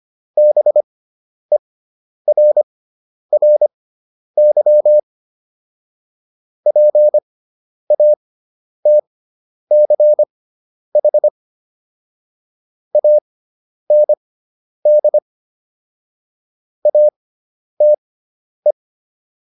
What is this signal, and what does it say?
Morse code, heard as a keyed tone, about 25 wpm, spaced as for 10 wpm, 600 Hz. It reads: BERRY PATCH AND ATE